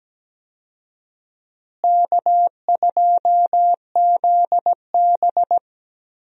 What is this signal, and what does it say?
Morse code, heard as a keyed tone, about 17 wpm, 700 Hz. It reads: K2ZB
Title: USA Call Signs - 4 Characters - Repeated 3x - 17wpm